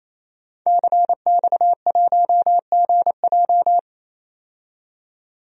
Morse code, heard as a keyed tone, about 28 wpm, 700 Hz. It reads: CX1GJ
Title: DXCC-100 Call Signs - 5 Characters - 28wpm